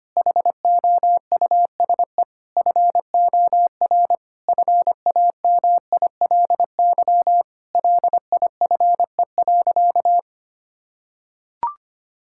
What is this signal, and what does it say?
Morse code, heard as a keyed tone, about 25 wpm, 700 Hz. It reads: HOUSE FOR FAMILY LIFE.
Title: Sentences from Top 1000 Words - Review Part 5 - 25wpm